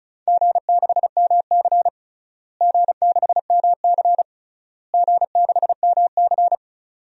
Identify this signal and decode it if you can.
Morse code, heard as a keyed tone, about 34 wpm, 700 Hz. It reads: G6MC G6MC G6MC
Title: DXCC-100 Call Signs - 4 Characters - Repeated 3x - 35wpm